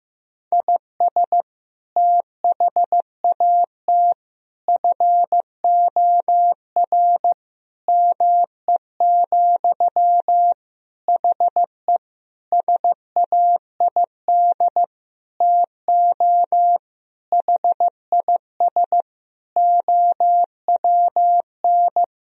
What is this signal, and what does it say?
Morse code, heard as a keyed tone, about 15 wpm, 700 Hz. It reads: IS THAT FOR ME, HE SAID TO HIS OWN
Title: Sentences from Top 1000 Words - Review Part 1 - 15wpm